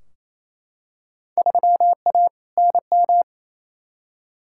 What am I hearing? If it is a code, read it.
Morse code, heard as a keyed tone, about 28 wpm, 700 Hz. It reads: 3A NM
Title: Field Day Exchanges Encore - 28wpm